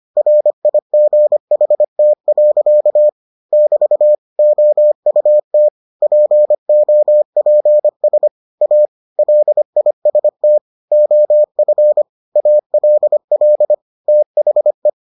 Morse code, heard as a keyed tone, about 25 wpm, 600 Hz. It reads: RIGHT. <BT> OUT POPS A LIST OF ALL THE